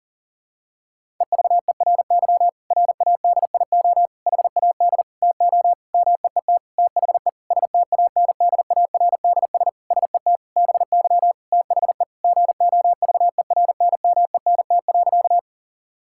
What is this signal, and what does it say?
Morse code, heard as a keyed tone, about 40 wpm, 700 Hz. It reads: EVERY RADIO HAD TO MEET THE STANDARDS SET BY THE GOVERNMENT.
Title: Sentences from Top 1000 Words - Review Part 6 - 40wpm